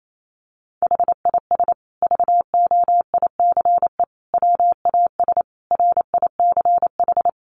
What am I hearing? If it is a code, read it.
Morse code, heard as a keyed tone, about 28 wpm, 700 Hz. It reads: HIS VOICE WAS RICH
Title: Sentences from Top 1100 Words 28wpm